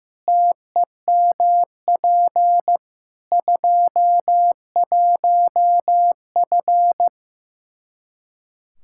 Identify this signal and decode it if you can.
Morse code, heard as a keyed tone, about 15 wpm, 700 Hz. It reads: TEMP 21F
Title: Practice QSO - Exchange 2 - Info - 15wpm